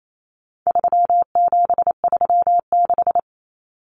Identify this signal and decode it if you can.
Morse code, heard as a keyed tone, about 28 wpm, 700 Hz. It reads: 3736
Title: Numbers - 4-Digit 28wpm